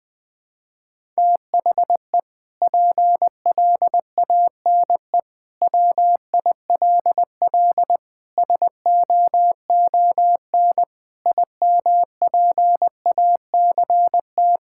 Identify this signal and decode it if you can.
Morse code, heard as a keyed tone, about 21 wpm, 700 Hz. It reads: THE PLANE WILL SOON IMPACT
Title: Sentences from Top 1000 Words - Review Part 6 - 20wpm